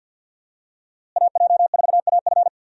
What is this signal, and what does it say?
Morse code, heard as a keyed tone, about 50 wpm, 700 Hz. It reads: AJ4AF
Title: USA Call Signs - 5 Characters - Repeated 3x - 50wpm